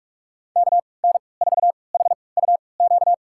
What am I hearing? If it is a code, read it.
Morse code, heard as a keyed tone, about 42 wpm, spaced as for 25 wpm, 690 Hz. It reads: KN4HUQ
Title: USA Call Signs - 6 Characters - 3x Character Spacing - 45wpm